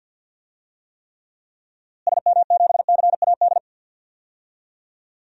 Morse code, heard as a keyed tone, about 52 wpm, 700 Hz. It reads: SM7CAD
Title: DXCC-100 Call Signs - 6 Characters - Repeated 2x - 50wpm